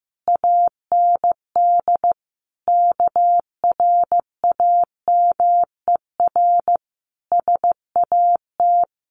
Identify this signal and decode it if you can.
Morse code, heard as a keyed tone, about 15 wpm, 700 Hz. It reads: AND KRAMER SAT
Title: A Sound Of Thunder 15wpm